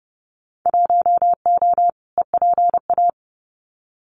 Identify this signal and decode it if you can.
Morse code, heard as a keyed tone, about 30 wpm, 700 Hz. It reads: 1O EPA
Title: Winter Field Day Exchanges 30wpm